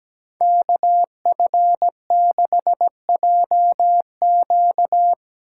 Morse code, heard as a keyed tone, about 19 wpm, 700 Hz. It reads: KF6JQ
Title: Speed Racing - USA 2x2 Call Signs 17wpm